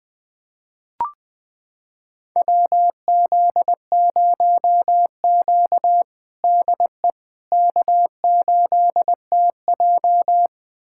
Morse code, heard as a keyed tone, about 20 wpm, 700 Hz. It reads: WZ0Q DE K8TJ
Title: USA Callsign DE Callsign - 4 Characters - 20wpm